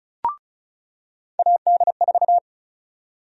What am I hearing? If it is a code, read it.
Morse code, heard as a keyed tone, about 34 wpm, 700 Hz. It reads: AD4
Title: USA Call Sign Prefixes - 35wpm